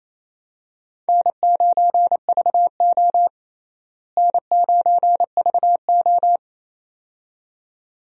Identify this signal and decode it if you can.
Morse code, heard as a keyed tone, about 28 wpm, 700 Hz. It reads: N9VO N9VO
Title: USA Call Signs - 4 Characters - Repeated 2x - 28wpm